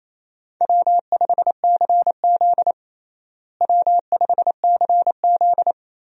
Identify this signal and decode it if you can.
Morse code, heard as a keyed tone, about 28 wpm, 700 Hz. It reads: W5CZ W5CZ
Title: USA Call Signs - 4 Characters - Repeated 2x - 28wpm